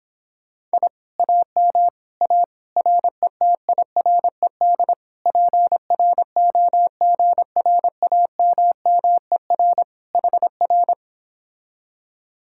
Morse code, heard as a keyed tone, about 26 wpm, 700 Hz. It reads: I AM A RETIRED PROGRAMMER HR